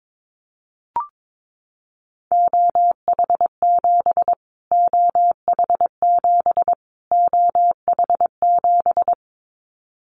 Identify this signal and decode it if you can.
Morse code, heard as a keyed tone, about 22 wpm, 700 Hz. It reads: OH7 OH7 OH7
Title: Callsign Prefixes for Top 25 DXCC - Repeated 3x - 22wpm